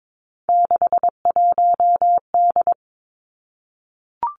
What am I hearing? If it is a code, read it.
Morse code, heard as a keyed tone, about 22 wpm, 700 Hz. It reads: 61D
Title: Three Challenging Letter-Number Combinations 22wpm